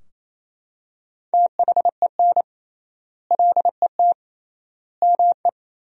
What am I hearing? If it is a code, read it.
Morse code, heard as a keyed tone, about 28 wpm, 700 Hz. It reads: THEN LET ME
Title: 3x Word-Spacing - Sentences from Top 300 Words - 28wpm